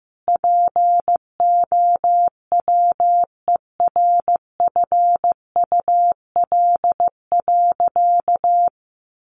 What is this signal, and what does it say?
Morse code, heard as a keyed tone, about 15 wpm, 700 Hz. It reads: POWERFUL.
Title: Sentences from Top 1300 Words 15wpm